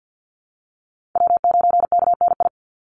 Morse code, heard as a keyed tone, about 50 wpm, 700 Hz. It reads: W9KNI
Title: Call Signs Repeated 3x - 50wpm